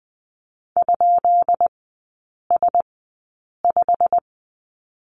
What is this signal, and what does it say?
Morse code, heard as a keyed tone, about 20 wpm, spaced as for 8 wpm, 690 Hz. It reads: ?S5
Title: 159 Lesson 26e - Letter g - Three Characters 20wpm